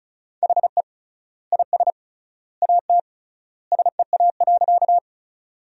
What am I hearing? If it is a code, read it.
Morse code, heard as a keyed tone, about 36 wpm, 700 Hz. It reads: HE IS AT SEA.
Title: 3x Word-Spacing - Sentences from Top 300 Words - 35wpm